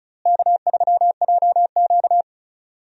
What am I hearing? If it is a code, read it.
Morse code, heard as a keyed tone, about 34 wpm, 700 Hz. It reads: K3JQ
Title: USA Call Signs - 4 Characters - Repeated 3x - 35wpm